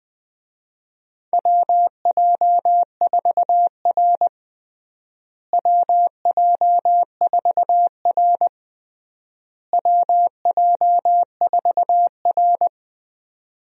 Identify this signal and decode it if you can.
Morse code, heard as a keyed tone, about 21 wpm, 700 Hz. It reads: WJ4R WJ4R WJ4R
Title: USA Call Signs - 4 Characters - Repeated 3x - 20wpm